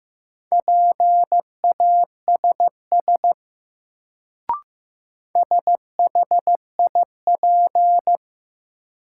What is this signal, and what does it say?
Morse code, heard as a keyed tone, about 15 wpm, 700 Hz. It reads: PASS SHIP
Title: Sentences from Top 400 Words - New Word Drill - 15wpm